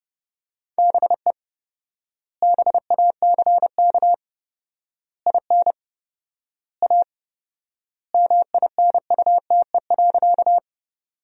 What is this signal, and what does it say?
Morse code, heard as a keyed tone, about 32 wpm, 700 Hz. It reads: BE BACK IN A MINUTE.
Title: 4x Word-Spacing - Sentences from Top 500 Words - 30wpm